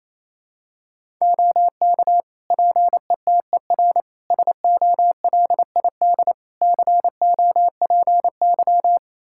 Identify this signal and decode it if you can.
Morse code, heard as a keyed tone, about 28 wpm, 700 Hz. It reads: OK PETER SOLID COPY
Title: K1USN SST QSO Practice 28wpm